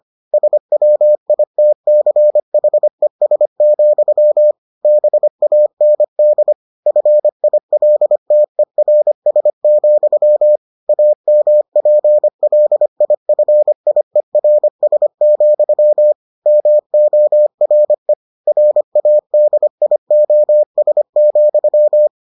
Morse code, heard as a keyed tone, about 25 wpm, 600 Hz. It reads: SWITCHES, BAND FILTERS, AMPLIFIERS, MORE RADIOS,